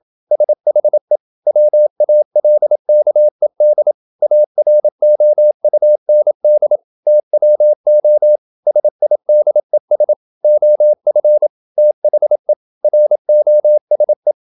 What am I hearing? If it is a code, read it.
Morse code, heard as a keyed tone, about 27 wpm, 600 Hz. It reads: SHE WALKED AROUND TWO SIDES OF THE ROSE
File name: Never 0027wpm